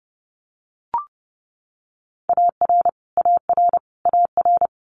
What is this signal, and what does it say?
Morse code, heard as a keyed tone, about 30 wpm, 700 Hz. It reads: AR AR AR
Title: Multi-letter Phonemes Repeated 3x 30wpm